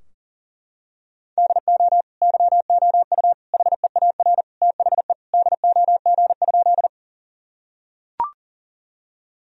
Morse code, heard as a keyed tone, about 40 wpm, 700 Hz. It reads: DO YOU HEAR THE DOG?
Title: Sentences from Top 1000 Words - Review Part 2 - 40wpm